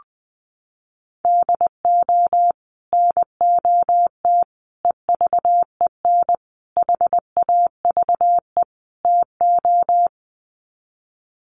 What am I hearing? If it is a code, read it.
Morse code, heard as a keyed tone, about 20 wpm, 700 Hz. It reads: DO NOT EVEN HAVE TO
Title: Sets of 5 Words Encore2 20wpm